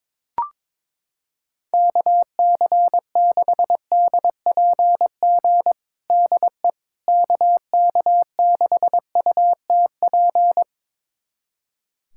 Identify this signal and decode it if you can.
Morse code, heard as a keyed tone, about 21 wpm, 700 Hz. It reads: KC6DPG DE KK6UTP
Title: USA Callsign DE Callsign - 6 Characters - 22wpm